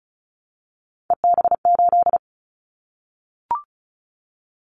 Morse code, heard as a keyed tone, about 36 wpm, 700 Hz. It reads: EB8
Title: Callsign Prefixes for Top 100 DXCC - 35wpm